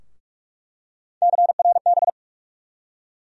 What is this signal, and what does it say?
Morse code, heard as a keyed tone, about 43 wpm, 700 Hz. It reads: CRB
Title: USA Call Sign Postfixes - 3 Letters 45wpm